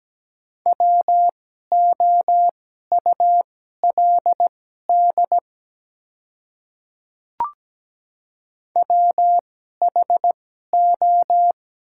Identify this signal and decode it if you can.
Morse code, heard as a keyed tone, about 17 wpm, spaced as for 12 wpm, 700 Hz. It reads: WOULD WHO
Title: 2x Character Spacing - Top 200 Words - 17wpm